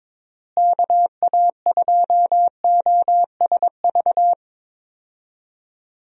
Morse code, heard as a keyed tone, about 22 wpm, 700 Hz. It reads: KA2OSV
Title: USA Call Signs - 6 Characters - Repeated 2x - 22wpm